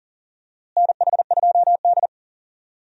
Morse code, heard as a keyed tone, about 40 wpm, 700 Hz. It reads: NH2D
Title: USA Call Signs - 4 Characters - Repeated 3x - 40wpm